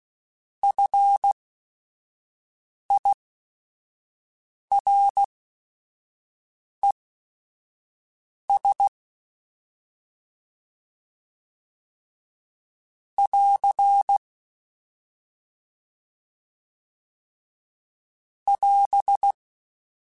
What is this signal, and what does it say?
Morse code, heard as a keyed tone, about 16 wpm, spaced as for 5 wpm, 800 Hz. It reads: FIRES <AR> <AS>